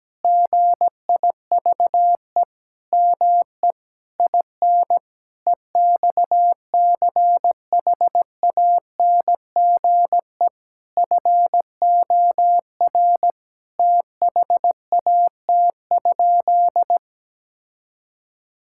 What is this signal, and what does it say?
Morse code, heard as a keyed tone, about 17 wpm, 700 Hz. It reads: GIVE ME IN EXCHANGE FOR THAT?
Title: Sentences from Top 1000 Words - Review Part 6 - 17wpm